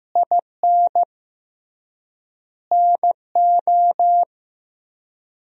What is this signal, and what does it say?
Morse code, heard as a keyed tone, about 15 wpm, 700 Hz. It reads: IN NO
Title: 3x Word-Spacing - Sentences from Top 400 Words - 15wpm